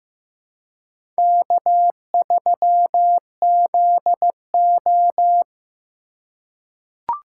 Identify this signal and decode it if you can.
Morse code, heard as a keyed tone, about 15 wpm, 700 Hz. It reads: K3ZO
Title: USA Call Signs - 4 Characters - 2x Character Spacing - 15wpm